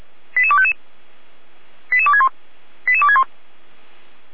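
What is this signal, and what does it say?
Five-tone selective call (ZVEI), heard as quick runs of five tones, tones of about 70 ms. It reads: 88255 88261 88261
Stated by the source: ZVEI five-tone selective call received by SDR.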